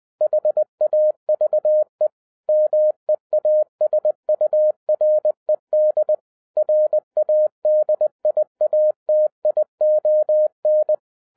Morse code, heard as a keyed tone, about 20 wpm, 610 Hz. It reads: HAVE MEASURED RADIATION
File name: 2011-10-mp3 / n03-TARO JR0QWW 20wpm